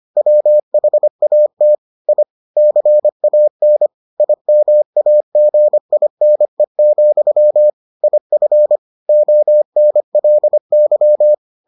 Morse code, heard as a keyed tone, about 25 wpm, 600 Hz. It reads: WHAT I CAN IMAGINE, IF ONLY